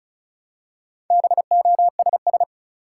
Morse code, heard as a keyed tone, about 36 wpm, 700 Hz. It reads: BOSS
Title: Four Letter Words - 35wpm